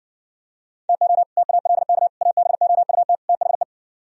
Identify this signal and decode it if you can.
Morse code, heard as a keyed tone, about 69 wpm, 700 Hz. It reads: TO TALK ABOUT THE